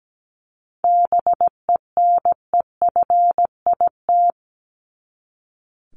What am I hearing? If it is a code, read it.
Morse code, heard as a keyed tone, about 17 wpm, 700 Hz. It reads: BENEFIT
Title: Top 751-1000 Words 17wpm